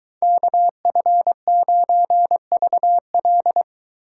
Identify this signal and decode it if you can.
Morse code, heard as a keyed tone, about 23 wpm, 700 Hz. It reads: KF9VL